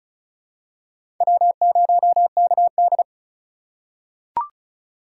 Morse code, heard as a keyed tone, about 36 wpm, 700 Hz. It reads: W0KD
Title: USA Call Signs - 4 Characters - 4x Character Spacing - 35wpm